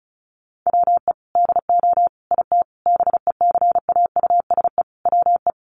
Morse code, heard as a keyed tone, about 34 wpm, 700 Hz. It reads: WE DO IT BECAUSE WE
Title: Sentences from Top 1400 Words 35wpm